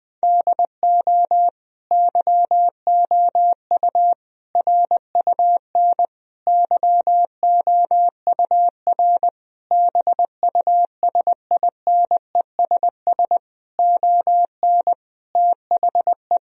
Morse code, heard as a keyed tone, about 21 wpm, 700 Hz. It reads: DO YOU RUN YOUR BUSINESS ON THE
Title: Sentences from Top 1000 Words - Review Part 4 - 20wpm